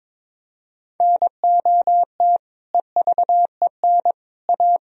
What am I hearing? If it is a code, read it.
Morse code, heard as a keyed tone, about 22 wpm, 700 Hz. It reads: NOT EVEN A
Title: Sentences from Top 1000 Words - Review Part 1 - 22wpm